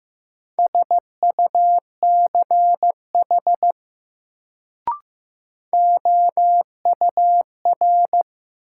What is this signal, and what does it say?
Morse code, heard as a keyed tone, about 15 wpm, 700 Hz. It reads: SUCH OUR
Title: Sentences from Top 200 Words - New Word Drill - 15wpm